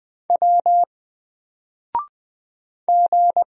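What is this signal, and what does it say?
Morse code, heard as a keyed tone, about 20 wpm, 700 Hz. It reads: W G
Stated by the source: Single letter and number Morse drill at 20 wpm, from N4LCD.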